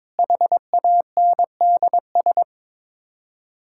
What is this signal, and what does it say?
Morse code, heard as a keyed tone, about 21 wpm, 700 Hz. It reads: HANDS
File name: practice / 22wpm / Sets of 8 Words - 22wpm